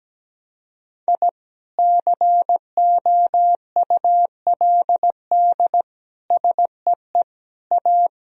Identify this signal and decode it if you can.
Morse code, heard as a keyed tone, about 17 wpm, 700 Hz. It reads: I COULD SEE A
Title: Sentences from Top 1000 Words - Review Part 2 - 17wpm